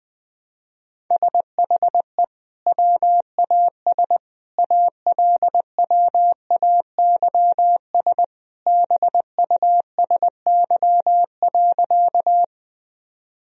Morse code, heard as a keyed tone, about 21 wpm, 700 Hz. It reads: SHE WAS ALWAYS BUSY.
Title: Sentences from Top 1000 Words - Review Part 3 - 20wpm